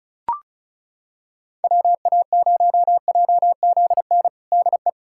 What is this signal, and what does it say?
Morse code, heard as a keyed tone, about 36 wpm, 700 Hz. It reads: WA0JZN DE
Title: USA Callsign DE Callsign 35wpm